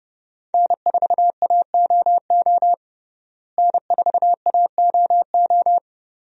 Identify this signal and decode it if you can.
Morse code, heard as a keyed tone, about 30 wpm, 700 Hz. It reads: N4AOO N4AOO
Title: USA Call Signs - 5 Characters - Repeated 2x - 30wpm